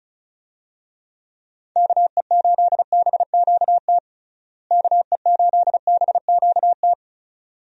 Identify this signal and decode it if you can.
Morse code, heard as a keyed tone, about 36 wpm, 700 Hz. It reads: KE8BQT KE8BQT
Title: USA Call Signs - 6 Characters - Repeated 2x - 35wpm